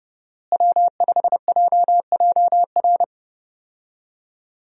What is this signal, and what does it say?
Morse code, heard as a keyed tone, about 30 wpm, 700 Hz. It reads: W5JJR
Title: USA Call Signs - 5 Characters - Repeated 3x - 30wpm